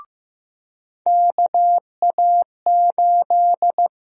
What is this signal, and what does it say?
Morse code, heard as a keyed tone, about 15 wpm, 700 Hz. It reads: KA8
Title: USA Call Sign Prefixes - Repeated 2x - 15wpm